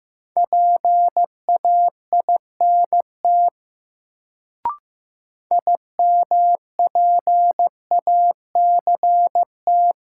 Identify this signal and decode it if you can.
Morse code, heard as a keyed tone, about 15 wpm, 700 Hz. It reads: PAINT IMPACT
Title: Sentences from Top 1000 Words - Review Part 6 - New Word Drill - 15wpm